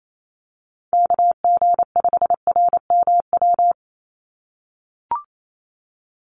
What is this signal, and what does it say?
Morse code, heard as a keyed tone, about 28 wpm, 700 Hz. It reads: KG5RMW
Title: USA Call Signs 28wpm